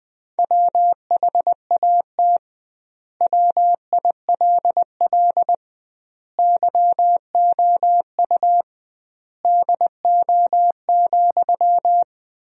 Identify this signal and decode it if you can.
Morse code, heard as a keyed tone, about 20 wpm, 700 Hz. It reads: WHAT WILL YOU DO,
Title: 2x Word-Spacing - Sentences from Top 100 Words - 20wpm